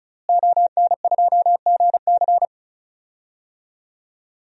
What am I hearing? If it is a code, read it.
Morse code, heard as a keyed tone, about 36 wpm, 700 Hz. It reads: ON2GC
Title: DXCC-100 Call Signs - 5 Characters - Repeated 3x - 35wpm